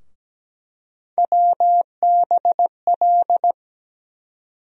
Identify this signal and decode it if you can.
Morse code, heard as a keyed tone, about 17 wpm, 700 Hz. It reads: WBL
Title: USA Call Sign Postfixes - 3 Letters 17wpm